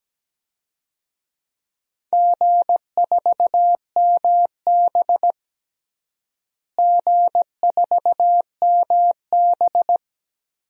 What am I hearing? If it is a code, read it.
Morse code, heard as a keyed tone, about 17 wpm, 700 Hz. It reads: G4MB G4MB
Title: DXCC-100 Call Signs - 4 Characters - Repeated 3x - 17wpm